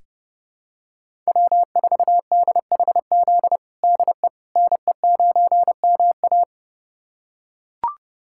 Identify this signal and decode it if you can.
Morse code, heard as a keyed tone, about 32 wpm, 700 Hz. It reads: W4DHZ DE NE9MA
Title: USA Callsign DE Callsign - 5 Characters - 30wpm